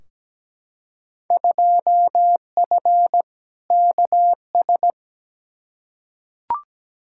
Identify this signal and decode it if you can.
Morse code, heard as a keyed tone, about 17 wpm, 700 Hz. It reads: 2F KS
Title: Field Day Exchanges Encore - 17wpm